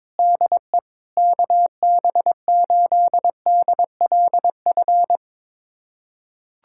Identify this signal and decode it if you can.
Morse code, heard as a keyed tone, about 22 wpm, 700 Hz. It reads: DE KB8DLF
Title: USA Callsign DE Callsign 22wpm